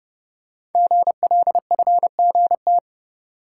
Morse code, heard as a keyed tone, about 28 wpm, 700 Hz. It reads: GLFGT
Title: Code Groups - 5 Letters - 30wpm